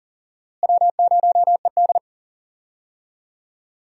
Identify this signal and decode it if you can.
Morse code, heard as a keyed tone, about 38 wpm, 700 Hz. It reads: W0ED
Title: USA Call Signs - 4 Characters - Repeated 3x - 40wpm